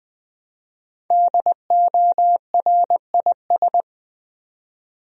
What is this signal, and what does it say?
Morse code, heard as a keyed tone, about 20 wpm, 700 Hz. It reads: DORIS
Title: USA Common Names - Max 5 Letters - 20wpm